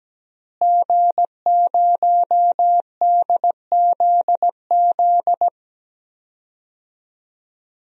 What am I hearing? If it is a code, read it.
Morse code, heard as a keyed tone, about 17 wpm, 700 Hz. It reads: G0DZZ
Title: DXCC-100 Call Signs - 5 Characters - Repeated 3x - 17wpm